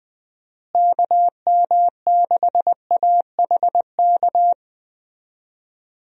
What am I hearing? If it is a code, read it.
Morse code, heard as a keyed tone, about 21 wpm, 700 Hz. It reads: KM6AHK
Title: USA Call Signs - 6 Characters - Repeated 3x - 20wpm